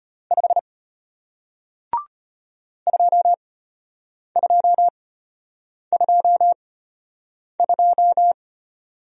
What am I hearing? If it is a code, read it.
Morse code, heard as a keyed tone, about 34 wpm, spaced as for 8 wpm, 700 Hz. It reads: 5 2222